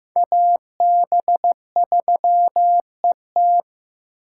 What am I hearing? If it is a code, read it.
Morse code, heard as a keyed tone, about 15 wpm, 700 Hz. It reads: AB3ET
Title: USA Call Signs - 5 Characters - 3x Character Spacing - 15wpm